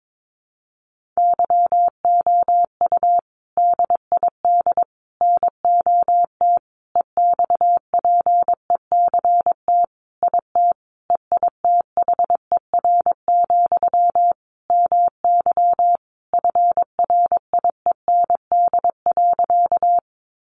Morse code, heard as a keyed tone, about 22 wpm, 700 Hz. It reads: YOU DID NOT EXPECT IT EITHER, MY FRIEND.
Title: Sentences from Top 1100 Words 22wpm